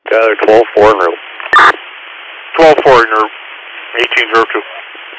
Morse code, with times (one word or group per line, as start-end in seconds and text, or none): none